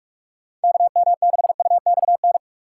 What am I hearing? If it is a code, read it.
Morse code, heard as a keyed tone, about 44 wpm, 700 Hz. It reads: KM6UXN
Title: USA Call Signs - 6x Character Spacing - 45wpm